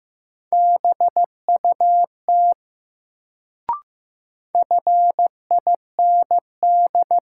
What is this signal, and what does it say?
Morse code, heard as a keyed tone, about 15 wpm, 700 Hz. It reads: BUT FIND
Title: Sets of 2 Words - New Word Drill - 15wpm